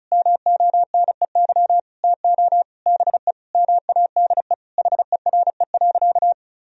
Morse code, heard as a keyed tone, about 36 wpm, 700 Hz. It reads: MONEY TO BE MADE HERE.
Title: Sentences from Top 1000 Words - Review Part 4 - 35wpm